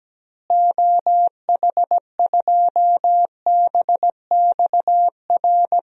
Morse code, heard as a keyed tone, about 17 wpm, 700 Hz. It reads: OH2BXR
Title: DXCC-100 Call Signs - 6 Characters - Repeated 2x - 17wpm